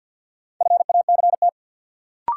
Morse code, heard as a keyed tone, about 53 wpm, 700 Hz. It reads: FACT
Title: Sentences from Top 500 Words - New Word Drill - 50wpm